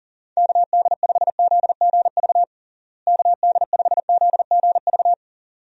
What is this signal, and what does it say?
Morse code, heard as a keyed tone, about 40 wpm, 700 Hz. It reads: KD5ZGV KD5ZGV
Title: USA Call Signs - 6 Characters - Repeated 2x - 40wpm